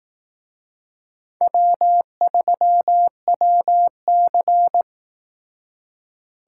Morse code, heard as a keyed tone, about 18 wpm, 700 Hz. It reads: W3WC
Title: Speed Racing - USA 1x2 Call Signs 15wpm